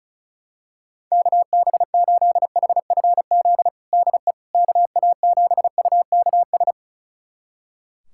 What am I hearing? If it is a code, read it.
Morse code, heard as a keyed tone, about 34 wpm, 700 Hz. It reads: KB8HFZ DE KA7UKS
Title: USA Callsign DE Callsign - 6 Characters - 35wpm